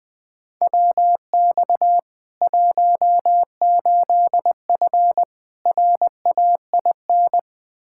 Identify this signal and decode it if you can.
Morse code, heard as a keyed tone, about 20 wpm, 700 Hz. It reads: WX 18F RAIN
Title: Practice QSO - Exchange 2 - Info - 20wpm